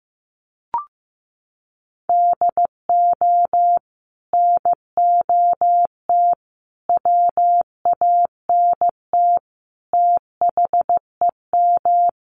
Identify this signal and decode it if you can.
Morse code, heard as a keyed tone, about 14 wpm, 700 Hz. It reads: DO NOT WANT THEM
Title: Sets of 5 Words Encore 15wpm